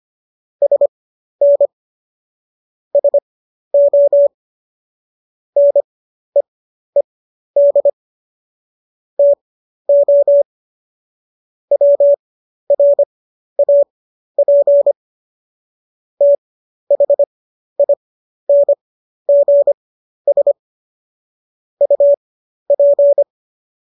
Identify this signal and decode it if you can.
Morse code, heard as a keyed tone, about 25 wpm, spaced as for 12 wpm, 570 Hz. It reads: SN SO NEED TO WRAP THINGS UP